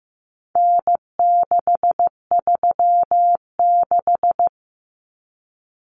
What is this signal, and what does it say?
Morse code, heard as a keyed tone, about 15 wpm, 700 Hz. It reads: N636
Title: Common Cut Numbers - 4-Digit 15wpm